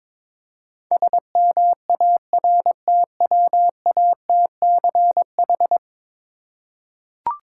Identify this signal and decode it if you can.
Morse code, heard as a keyed tone, about 22 wpm, 700 Hz. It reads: SMARTWATCH
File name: Difficult Words with Hints - 22wpm